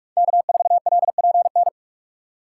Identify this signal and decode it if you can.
Morse code, heard as a keyed tone, about 46 wpm, 700 Hz. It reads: K4LPN
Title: USA Call Signs - 5 Characters - 6x Character Spacing - 45wpm